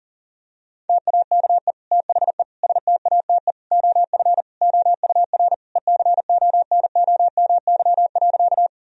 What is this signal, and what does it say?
Morse code, heard as a keyed tone, about 40 wpm, 700 Hz. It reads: TAKE THE STATE OF OUR ECONOMY.
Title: Sentences from Top 1000 Words - Review Part 3 - 40wpm